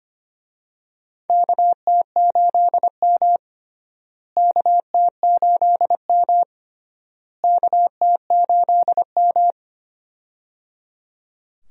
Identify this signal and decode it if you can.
Morse code, heard as a keyed tone, about 25 wpm, 700 Hz. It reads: KT8M KT8M KT8M
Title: USA Call Signs - 4 Characters - Repeated 3x - 25wpm